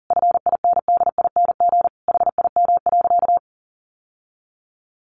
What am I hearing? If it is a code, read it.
Morse code, heard as a keyed tone, about 40 wpm, 700 Hz. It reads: FINDING HIM.
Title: Sentences from Top 1100 Words 40wpm